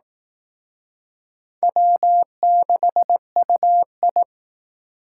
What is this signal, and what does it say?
Morse code, heard as a keyed tone, about 18 wpm, 700 Hz. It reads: W6UI
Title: Speed Racing - USA 1x2 Call Signs 15wpm